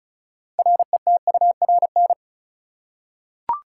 Morse code, heard as a keyed tone, about 36 wpm, 700 Hz. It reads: RETURN
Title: Six Letter Words - 35wpm